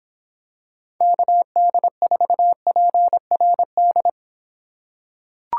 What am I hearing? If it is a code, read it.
Morse code, heard as a keyed tone, about 26 wpm, 700 Hz. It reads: KD4PRD